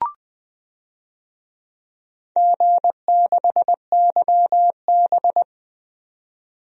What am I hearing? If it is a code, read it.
Morse code, heard as a keyed tone, about 20 wpm, 700 Hz. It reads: G6YB
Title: DXCC-100 Call Signs - 4 Characters - 20wpm